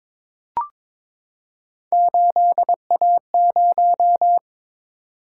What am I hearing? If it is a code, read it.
Morse code, heard as a keyed tone, about 22 wpm, 700 Hz. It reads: 8A0
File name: practice / 22wpm / Three Letter-Number Combinations - 22wpm